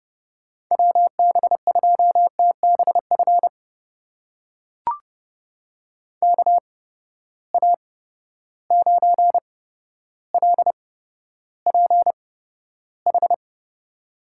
Morse code, heard as a keyed tone, about 31 wpm, spaced as for 13 wpm, 700 Hz. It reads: WB2TBF KA9LPH
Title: USA Call Signs - 6 Characters - 8x Character Spacing - 30wpm